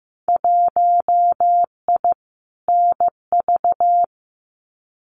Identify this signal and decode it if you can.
Morse code, heard as a keyed tone, about 15 wpm, 700 Hz. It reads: 1I NV
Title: Winter Field Day Exchanges Encore 15wpm